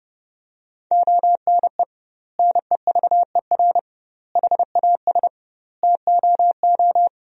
Morse code, heard as a keyed tone, about 30 wpm, 700 Hz. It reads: ONE NEVER HAS TOO
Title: 2x Word-Spacing - Sentences from Top 300 Words - 30wpm